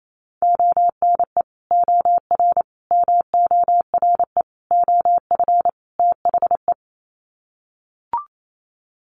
Morse code, heard as a keyed tone, about 28 wpm, 700 Hz. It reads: ONE OR MORE OF THE
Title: Sets of 5 Words Encore 28wpm